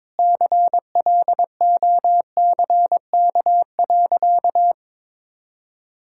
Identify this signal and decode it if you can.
Morse code, heard as a keyed tone, about 22 wpm, 700 Hz. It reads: CLOCK.
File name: practice / 22wpm / Easy Sentences with Hints - 22wpm